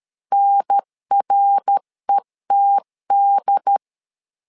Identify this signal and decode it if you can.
Morse code, heard as a keyed tone, about 13 wpm, 800 Hz. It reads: NRETD
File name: LEZIONI / 3 MWRO / 3 MWR 12wpm 1711